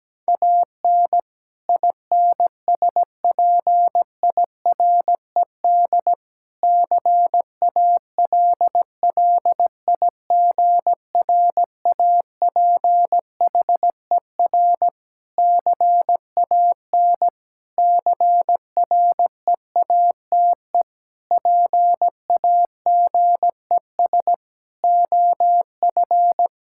Morse code, heard as a keyed tone, about 17 wpm, 700 Hz. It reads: AN INSPIRED CALLIGRAPHER CAN CREATE PAGES OF
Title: Pangram Challenge v2 - 17wpm